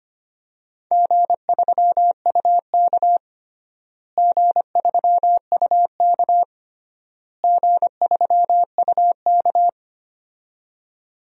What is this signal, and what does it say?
Morse code, heard as a keyed tone, about 25 wpm, 700 Hz. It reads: G3UK G3UK G3UK
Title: DXCC-100 Call Signs - 4 Characters - Repeated 3x - 25wpm